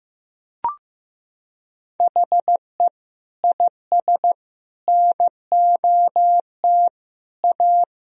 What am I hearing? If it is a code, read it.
Morse code, heard as a keyed tone, about 15 wpm, 700 Hz. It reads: HE IS NOT A
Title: Sentences from Top 1000 Words - Review Part 2 - 15wpm